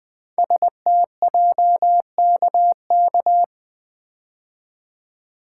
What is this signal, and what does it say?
Morse code, heard as a keyed tone, about 20 wpm, 700 Hz. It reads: STJKK
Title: Code Groups - 5 Letters - 20wpm